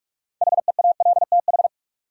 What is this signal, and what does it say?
Morse code, heard as a keyed tone, about 41 wpm, 700 Hz. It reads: HEALTH
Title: Sets of 5 Words - New Word Drill - 45wpm